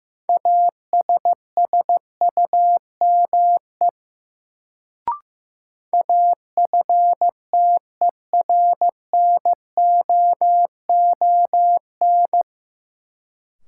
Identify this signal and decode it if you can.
Morse code, heard as a keyed tone, about 15 wpm, 700 Hz. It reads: ASSUME AFTERNOON
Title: Sentences from Top 1500 Words - New Word Drill Alt - 15wpm